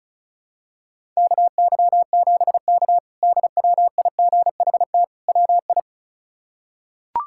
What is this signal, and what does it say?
Morse code, heard as a keyed tone, about 36 wpm, 700 Hz. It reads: KY7K DWIGHT WI
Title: K1USN SST QSO Practice - 35wpm